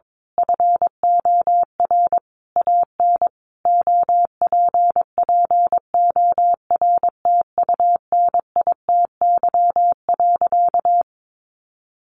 Morse code, heard as a keyed tone, about 22 wpm, 700 Hz. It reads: FOR AN OPPORTUNITY.
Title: Sentences from Top 1000 Words 22wpm